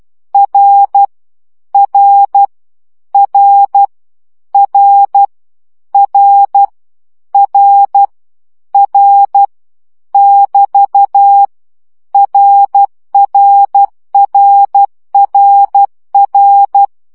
Morse code, heard as a keyed tone, about 12 wpm, 800 Hz. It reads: R R R R R R R <BT> RRRRR
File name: CW gyakorlas r 10 baud